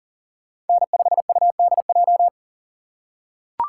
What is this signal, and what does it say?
Morse code, heard as a keyed tone, about 40 wpm, 700 Hz. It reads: N5UDJ
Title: USA Call Signs - 5 Characters - 8x Character Spacing - 40wpm